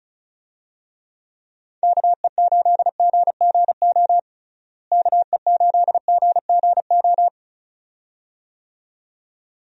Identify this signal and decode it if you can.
Morse code, heard as a keyed tone, about 34 wpm, 700 Hz. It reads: KE8GGO KE8GGO
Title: USA Call Signs - 6 Characters - Repeated 2x - 35wpm